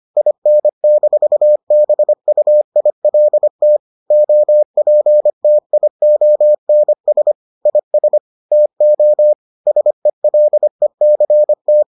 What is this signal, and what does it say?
Morse code, heard as a keyed tone, about 25 wpm, 600 Hz. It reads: IN-BUILT OPTIONS IS TO SELECT